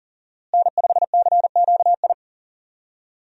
Morse code, heard as a keyed tone, about 40 wpm, 700 Hz. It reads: N5CQI
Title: USA Call Signs - 5 Characters - Repeated 3x - 40wpm